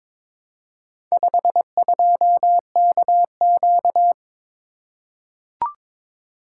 Morse code, heard as a keyed tone, about 22 wpm, 700 Hz. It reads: 52KQ
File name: practice / 22wpm / Four Letter-Number Combinations - 22wpm